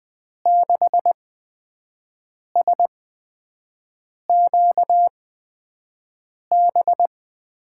Morse code, heard as a keyed tone, about 20 wpm, spaced as for 5 wpm, 700 Hz. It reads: 6SQB
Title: USA Call Signs - 8x Character Spacing - 20wpm